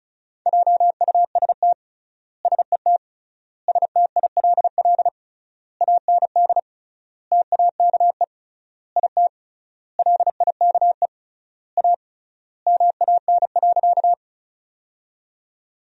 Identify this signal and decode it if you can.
Morse code, heard as a keyed tone, about 36 wpm, 700 Hz. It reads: JUST SET STILL AND TAKE IT LIKE A MAN.
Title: 3x Word-Spacing - Sentences from Top 300 Words - 35wpm